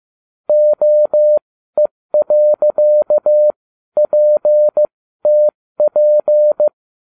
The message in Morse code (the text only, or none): OE.PTP